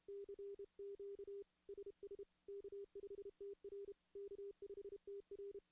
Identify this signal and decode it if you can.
Morse code, heard as a keyed tone, about 32 wpm, 400 Hz. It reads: CQ SS K5TR K5TR
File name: contests / CW-SS-k5tr- -2100